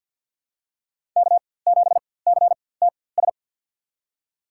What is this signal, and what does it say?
Morse code, heard as a keyed tone, about 51 wpm, spaced as for 23 wpm, 690 Hz. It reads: K7CTS
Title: USA Call Signs - 5 Characters - 4x Character Spacing - 50wpm